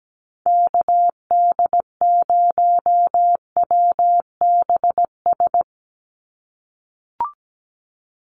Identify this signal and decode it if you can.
Morse code, heard as a keyed tone, about 17 wpm, 700 Hz. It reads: KD0WBS
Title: USA Call Signs 17wpm